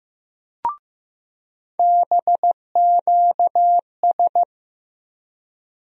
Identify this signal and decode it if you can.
Morse code, heard as a keyed tone, about 15 wpm, 700 Hz. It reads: BQS
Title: USA Call Sign Postfixes - 3 Letters 15wpm